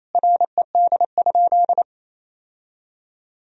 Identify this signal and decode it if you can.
Morse code, heard as a keyed tone, about 28 wpm, 700 Hz. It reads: RED?
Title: Sentences from Top 1000 Words - Review Part 2 - 28wpm